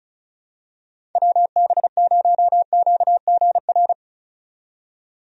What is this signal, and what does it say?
Morse code, heard as a keyed tone, about 34 wpm, 700 Hz. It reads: WB0QGR
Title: USA Call Signs - 6 Characters - 35wpm